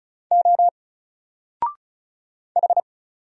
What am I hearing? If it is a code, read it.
Morse code, heard as a keyed tone, about 35 wpm, 700 Hz. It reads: O H
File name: Single-Letter-Number-35wpm